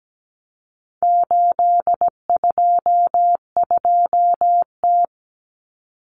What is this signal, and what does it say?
Morse code, heard as a keyed tone, about 17 wpm, 700 Hz. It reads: 822T
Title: Common Cut Numbers - 4-Digit 17wpm